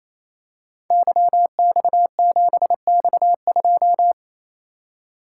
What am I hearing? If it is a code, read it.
Morse code, heard as a keyed tone, about 28 wpm, 700 Hz. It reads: YX7X2
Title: Code Groups - 5 Letter:Numbers - 28wpm